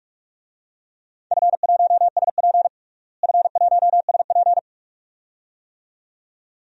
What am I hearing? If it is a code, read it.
Morse code, heard as a keyed tone, about 41 wpm, 700 Hz. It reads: F1SP F1SP
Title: DXCC-100 Call Signs - 4 Characters - Repeated 2x - 45wpm